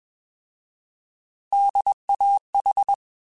Morse code, heard as a keyed tone, about 21 wpm, 800 Hz. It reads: DAH
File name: Alphabetical / Alphabetical mp3s / Ds @20